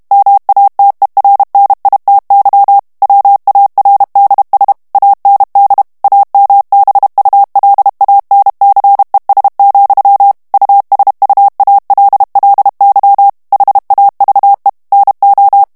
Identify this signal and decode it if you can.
Morse code, heard as a keyed tone, about 32 wpm, 800 Hz. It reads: MATERNITY WARDS AND AMBULANCES, USUALLY HAVE NO
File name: ANGLAIS texte clair 2 vitesse 30 mots minute